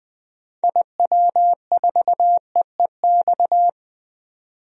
Morse code, heard as a keyed tone, about 20 wpm, 700 Hz. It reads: IW4EEX
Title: DXCC-100 Call Signs - 6 Characters - 20wpm